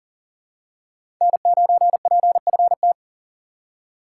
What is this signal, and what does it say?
Morse code, heard as a keyed tone, about 40 wpm, 700 Hz. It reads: N9PFT
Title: USA Call Signs - 5 Characters - 6x Character Spacing - 40wpm